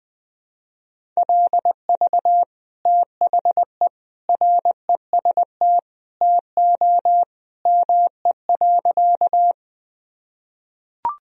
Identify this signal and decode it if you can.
Morse code, heard as a keyed tone, about 20 wpm, 700 Hz. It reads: LV THE REST TO ME.
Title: Sentences from Top 500 Words and Abbreviations 20wpm